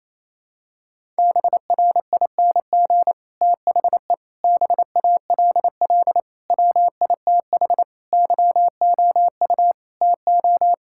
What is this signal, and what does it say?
Morse code, heard as a keyed tone, about 28 wpm, 700 Hz. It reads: BRING THE BALL WITH YOU TO
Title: Sentences from Top 1000 Words - Review Part 6 - 28wpm